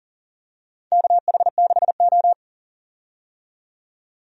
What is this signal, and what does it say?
Morse code, heard as a keyed tone, about 38 wpm, 700 Hz. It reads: KH6O
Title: DXCC-100 Call Signs - 4 Characters - 40wpm